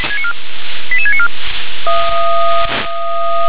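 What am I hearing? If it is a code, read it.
Five-tone selective call (ZVEI), heard as quick runs of five tones, tones of about 70 ms. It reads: 88694
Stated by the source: ZVEI five-tone selective call received on SDR.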